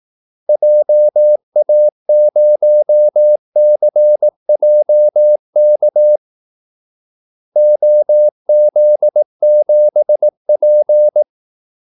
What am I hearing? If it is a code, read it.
Morse code, heard as a keyed tone, about 18 wpm, 600 Hz. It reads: JA0CJK OZ7P
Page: Kaldesignaler 18 wpm